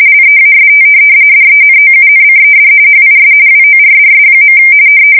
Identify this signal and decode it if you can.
RTTY, about 45.45 baud. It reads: LEAST DENOUNCED HIS WORDS.
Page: 170 Shift 2125/2295